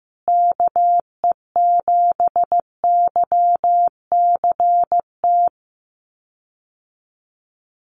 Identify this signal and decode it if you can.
Morse code, heard as a keyed tone, about 15 wpm, 700 Hz. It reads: KE7YCT
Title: USA Call Signs Repeated 2x 15wpm